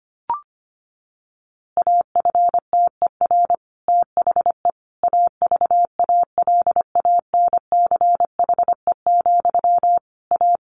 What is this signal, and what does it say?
Morse code, heard as a keyed tone, about 25 wpm, 700 Hz. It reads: AFTER THE AVALANCHE, A
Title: A Sound Of Thunder 25wpm